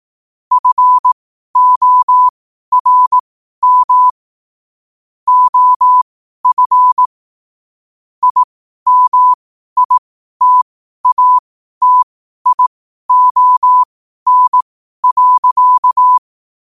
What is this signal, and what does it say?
Morse code, heard as a keyed tone, about 18 wpm, spaced as for 12 wpm, 1000 Hz. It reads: FORM OF IMITATION.
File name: qotdfiles / 12wpm